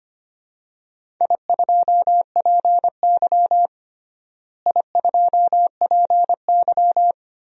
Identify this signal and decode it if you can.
Morse code, heard as a keyed tone, about 25 wpm, 700 Hz. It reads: I2PY I2PY
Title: DXCC-100 Call Signs - 4 Characters - Repeated 3x - 25wpm